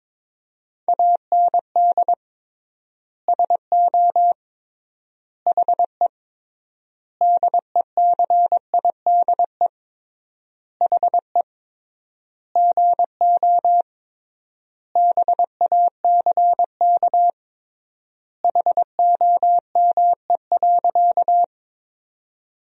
Morse code, heard as a keyed tone, about 21 wpm, 700 Hz. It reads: AND SO HE DECIDE HE GO BACK HOME.
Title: 3x Word-Spacing - Sentences from Top 500 Words - 22wpm